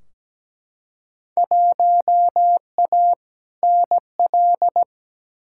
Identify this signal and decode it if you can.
Morse code, heard as a keyed tone, about 17 wpm, 700 Hz. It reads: 1A NL
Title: Field Day Exchanges Encore - 17wpm